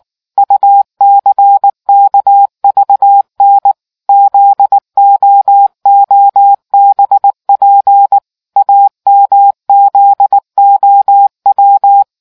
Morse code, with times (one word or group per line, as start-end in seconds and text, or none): 0.4-3.7 UCKVN
4.1-8.2 ZOOBP
8.6-12.0 AMZOW